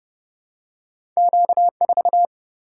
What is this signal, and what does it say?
Morse code, heard as a keyed tone, about 30 wpm, 700 Hz. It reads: Q 4
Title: Single-Letter Single-Number 30wpm